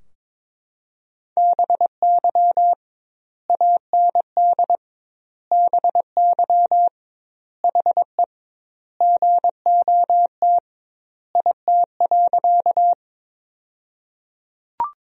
Morse code, heard as a keyed tone, about 21 wpm, 700 Hz. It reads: BY AND BY HE GOT IT.
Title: 2x Word-Spacing - Sentences from Top 300 Words - 22wpm